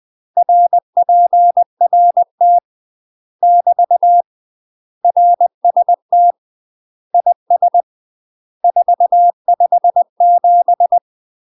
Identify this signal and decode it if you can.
Morse code, heard as a keyed tone, about 20 wpm, 700 Hz. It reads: RPRT <BT> RST IS 457
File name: UK MASTER QSOs TRAINING / 20WPM / UK MASTER QSOs 0070